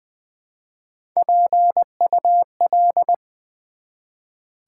Morse code, heard as a keyed tone, about 20 wpm, 700 Hz. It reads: PUL
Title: Build A Word - 1 Letter At A Time - 3 to 5-Letter Words - 20wpm